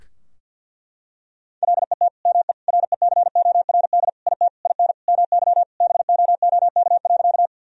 Morse code, heard as a keyed tone, about 50 wpm, 700 Hz. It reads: LET ME RECORD IT IN MY BOOK.
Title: Sentences from Top 1000 Words - Review Part 3 - 50wpm